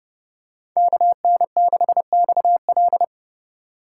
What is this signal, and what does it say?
Morse code, heard as a keyed tone, about 30 wpm, 700 Hz. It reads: KN6XL
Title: USA Call Signs - 4x Character Spacing - 30wpm